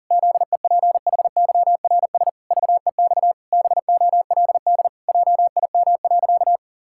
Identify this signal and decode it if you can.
Morse code, heard as a keyed tone, about 40 wpm, 700 Hz. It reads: ZEPHYRS VEX BOLD JIM.
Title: Pangram Challenge v2 - 40wpm